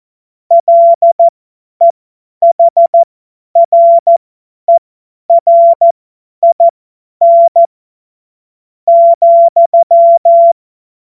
Morse code, heard as a keyed tone, about 14 wpm, spaced as for 9 wpm, 680 Hz. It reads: LEHRERIN,
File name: Morsekurs / Witze / Witz 14. März 2023 50 F70